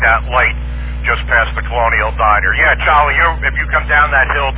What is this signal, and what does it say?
Emergency Alert System (SAME) tones, heard as harsh buzzing bursts, with speech. It is heard from 1.0 to 4.3 s.